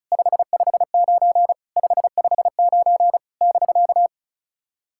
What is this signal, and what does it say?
Morse code, heard as a keyed tone, about 36 wpm, 700 Hz. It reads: 559 559 <BK>
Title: SOTA QSO Practice 35wpm